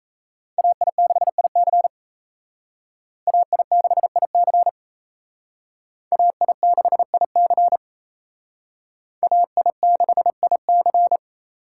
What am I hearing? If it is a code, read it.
Morse code, heard as a keyed tone, about 34 wpm, 700 Hz. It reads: AI6IC AI6IC AI6IC AI6IC
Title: Speed Racing - USA 2x2 Call Signs 28wpm